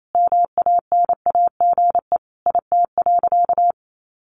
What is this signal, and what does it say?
Morse code, heard as a keyed tone, about 28 wpm, 700 Hz. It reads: MANAGE IT.
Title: Sentences from Top 1400 Words 28wpm